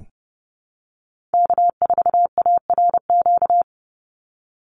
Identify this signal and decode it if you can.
Morse code, heard as a keyed tone, about 30 wpm, 700 Hz. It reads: K4ARQ
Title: Call Signs Repeated 3x - 30wpm